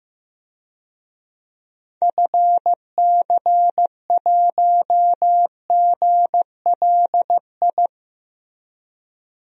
Morse code, heard as a keyed tone, about 15 wpm, 700 Hz. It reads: FC1GLI
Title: DXCC-100 Call Signs - 6 Characters - Repeated 2x - 15wpm